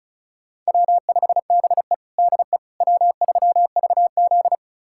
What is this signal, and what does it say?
Morse code, heard as a keyed tone, about 34 wpm, 700 Hz. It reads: W5BE DE W3VZ
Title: USA Callsign DE Callsign - 4 Characters - 35wpm